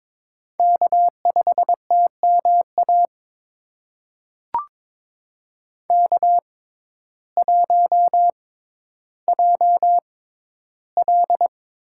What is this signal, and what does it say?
Morse code, heard as a keyed tone, about 21 wpm, spaced as for 12 wpm, 700 Hz. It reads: K5TMA K1JL
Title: USA Call Signs - 5 Characters - 6x Character Spacing - 22wpm